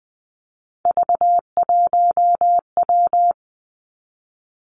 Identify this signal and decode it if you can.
Morse code, heard as a keyed tone, about 20 wpm, 700 Hz. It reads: V1W
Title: Three Challenging Letter-Number Combinations 20wpm